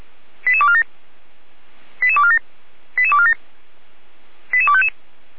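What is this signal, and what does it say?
Five-tone selective call (ZVEI), heard as quick runs of five tones, tones of about 70 ms. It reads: 88247 88247 88247 88255
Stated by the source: ZVEI five-tone selective call received on an SDR.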